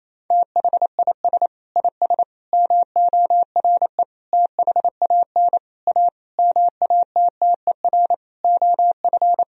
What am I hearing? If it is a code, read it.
Morse code, heard as a keyed tone, about 28 wpm, 700 Hz. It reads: THIS IS MORE THAN A MATTER OF
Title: Sentences from Top 1000 Words - Review Part 3 - 28wpm